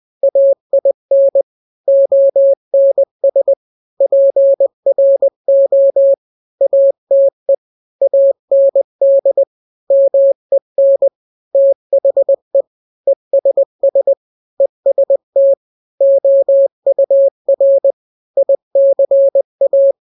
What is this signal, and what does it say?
Morse code, heard as a keyed tone, about 20 wpm, 550 Hz. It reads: AIN ONS PRO ATE AND MEN THE ESS EST OUR ICA
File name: n-grams / 3-grams 20 18 WPM